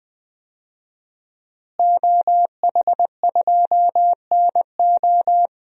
Morse code, heard as a keyed tone, about 20 wpm, 700 Hz. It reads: OH2NO
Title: DXCC-100 Call Signs - 5 Characters - Repeated 2x - 20wpm